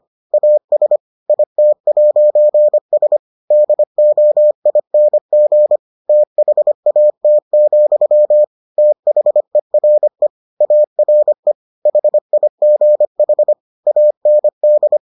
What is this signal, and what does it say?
Morse code, heard as a keyed tone, about 25 wpm, 600 Hz. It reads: AS IT'S DOING THAT, THERE ARE HIGH AND